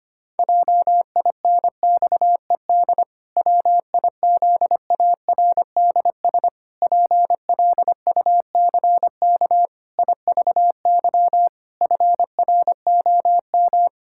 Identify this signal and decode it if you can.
Morse code, heard as a keyed tone, about 25 wpm, 700 Hz. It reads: JINXED WIZARDS PLUCK IVY FROM
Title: Pangram Challenge v2 - 25wpm